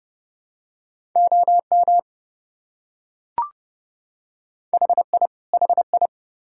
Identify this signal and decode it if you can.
Morse code, heard as a keyed tone, about 28 wpm, 700 Hz. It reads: OM HI HI